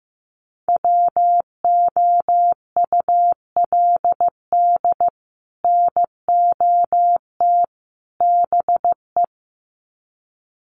Morse code, heard as a keyed tone, about 15 wpm, 700 Hz. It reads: WOULD NOT BE
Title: Sets of 3 Words 15wpm